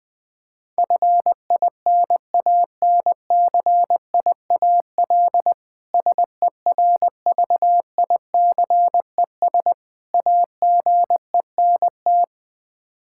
Morse code, heard as a keyed tone, about 20 wpm, 700 Hz. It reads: FINANCIAL SERVICES AGENT
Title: K1USN SST QSO Practice 20wpm